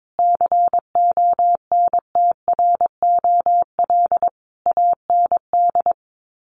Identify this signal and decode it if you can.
Morse code, heard as a keyed tone, about 22 wpm, 700 Hz. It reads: CONTROL AND
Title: Sets of 5 Words Encore2 22wpm